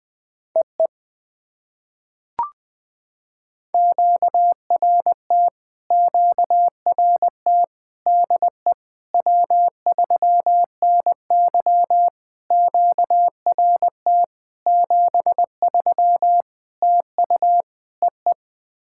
Morse code, heard as a keyed tone, about 20 wpm, 700 Hz. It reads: EE QRT QRT DE W3NY QRT 73 TU EE
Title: POTA QSO Practice - 20wpm